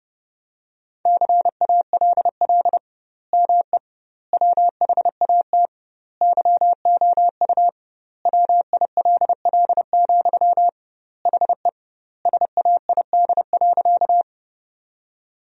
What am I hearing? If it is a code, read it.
Morse code, heard as a keyed tone, about 30 wpm, 700 Hz. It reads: CALL ME WHAT YOU WILL, HE SAID.
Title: 2x Word-Spacing - Sentences from Top 200 Words - 30wpm